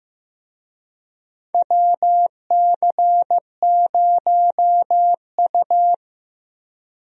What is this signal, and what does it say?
Morse code, heard as a keyed tone, about 15 wpm, 700 Hz. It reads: WC0U
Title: USA Call Signs - 4 Characters - Repeated 2x - 15wpm